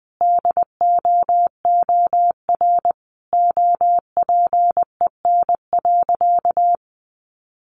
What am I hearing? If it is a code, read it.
Morse code, heard as a keyed tone, about 20 wpm, 700 Hz. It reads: DOOR OPEN.
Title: Sentences from Top 1000 Words 20wpm